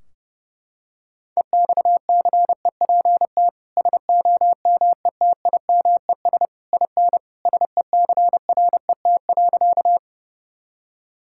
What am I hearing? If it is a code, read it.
Morse code, heard as a keyed tone, about 30 wpm, 700 Hz. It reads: EXCEPT SOMETIMES IN SECRET.
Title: Sentences from Top 1600 Words 30wpm